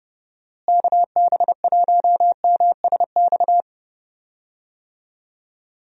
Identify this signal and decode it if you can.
Morse code, heard as a keyed tone, about 29 wpm, 700 Hz. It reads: KB1MSX
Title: USA Call Signs - 6 Characters - Repeated 3x - 30wpm